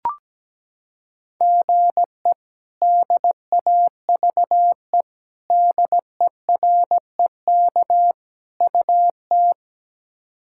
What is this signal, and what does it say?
Morse code, heard as a keyed tone, about 17 wpm, 700 Hz. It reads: GE DAVE DEREK UT
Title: K1USN SST QSO Practice - 17wpm